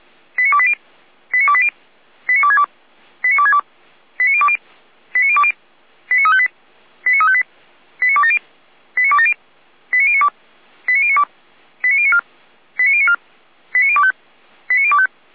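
Five-tone selective call (ZVEI), heard as quick runs of five tones, tones of about 70 ms. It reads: 89280 89280 89262 89262 89020 89020 89358 89358 89270 89270 89092 89092 89094 89094 89025 89025
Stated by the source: ZVEI five-tone call 89050 received on an SDR.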